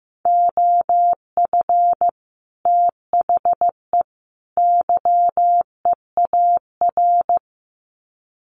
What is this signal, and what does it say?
Morse code, heard as a keyed tone, about 15 wpm, 700 Hz. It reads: OF THE YEAR
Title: Sets of 3 Words 15wpm